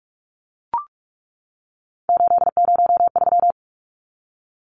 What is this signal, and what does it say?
Morse code, heard as a keyed tone, about 44 wpm, 700 Hz. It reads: 803
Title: Numbers - 3-Digit 45wpm